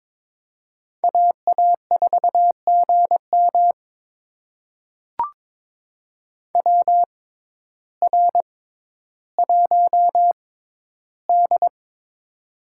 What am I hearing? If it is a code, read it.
Morse code, heard as a keyed tone, about 22 wpm, spaced as for 12 wpm, 700 Hz. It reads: AA4GM WR1D
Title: USA Call Signs - 6x Character Spacing - 22wpm